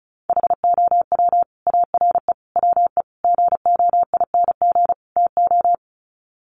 Morse code, heard as a keyed tone, about 34 wpm, 700 Hz. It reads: HOW ARE WE GOING TO
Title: Sets of 5 Words Encore 35wpm